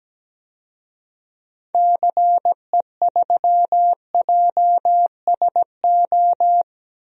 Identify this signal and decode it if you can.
Morse code, heard as a keyed tone, about 17 wpm, 700 Hz. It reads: CE3JSO
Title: DXCC-100 Call Signs - 6 Characters - Repeated 2x - 17wpm